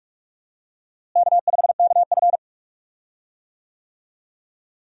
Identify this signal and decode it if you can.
Morse code, heard as a keyed tone, about 46 wpm, 700 Hz. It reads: K5KF
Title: USA Call Signs - 4 Characters - Repeated 3x - 45wpm